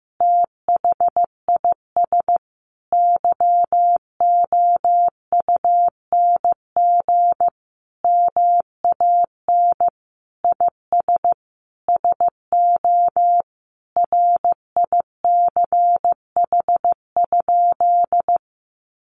Morse code, heard as a keyed tone, about 15 wpm, 700 Hz. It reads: THIS YOUNG MAN IS SO RICH?
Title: Sentences from Top 1100 Words 15wpm